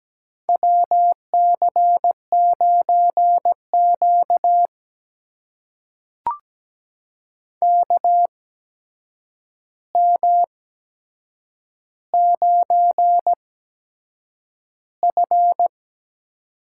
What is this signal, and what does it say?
Morse code, heard as a keyed tone, about 17 wpm, spaced as for 7 wpm, 700 Hz. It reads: WC9Q KM9F
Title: USA Call Signs - 4 Characters - 8x Character Spacing - 17wpm